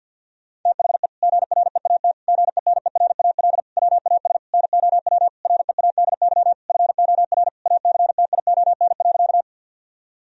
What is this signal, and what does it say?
Morse code, heard as a keyed tone, about 50 wpm, 700 Hz. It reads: THE GREAT GENERAL WAS NOW READY FOR ACTION.
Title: Sentences from Top 1000 Words - Review Part 4 - 50wpm